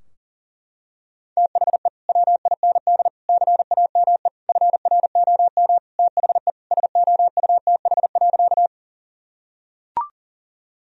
Morse code, heard as a keyed tone, about 40 wpm, 700 Hz. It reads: THE WIND CAME FROM THE SOUTH.
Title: Sentences from Top 1000 Words - Review Part 2 - 40wpm